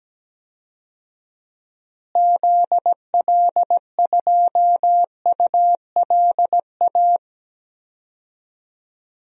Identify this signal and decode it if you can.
Morse code, heard as a keyed tone, about 17 wpm, 700 Hz. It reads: ZL2ULA
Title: DXCC-100 Call Signs - 6 Characters - Repeated 2x - 17wpm